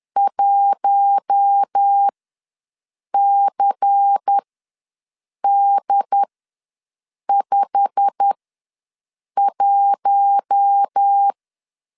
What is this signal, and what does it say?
Morse code, heard as a keyed tone, about 10 wpm, spaced as for 6 wpm, 800 Hz. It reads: 1CD51
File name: LEZIONI / 8 YJ45 / 8 YJ45 ws 10wpm 3843